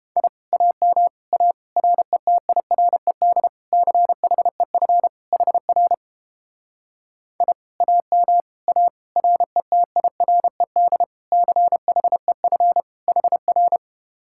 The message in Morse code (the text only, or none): I AM A RETIRED CHEF HR I AM A RETIRED CHEF HR